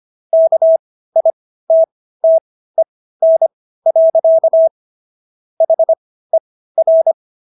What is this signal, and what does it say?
Morse code, heard as a keyed tone, about 25 wpm, spaced as for 15 wpm, 650 Hz. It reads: KITTEN. HER